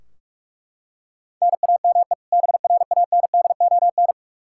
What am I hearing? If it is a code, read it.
Morse code, heard as a keyed tone, about 41 wpm, 700 Hz. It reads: NAME BRANDON
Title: K1USN SST QSO Practice 45wpm